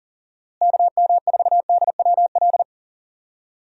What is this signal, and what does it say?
Morse code, heard as a keyed tone, about 37 wpm, 700 Hz. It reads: KM4DWL
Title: USA Call Signs - 6 Characters - Repeated 2x - 40wpm